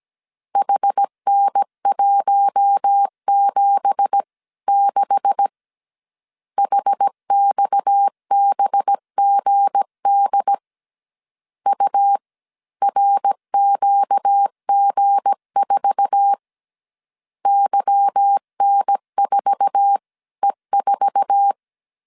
Morse code, heard as a keyed tone, about 17 wpm, 800 Hz. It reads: HN176 HXBGD URQG4 YN4E4
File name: CW / DET / 1 9 1500C S C 16wpm 2909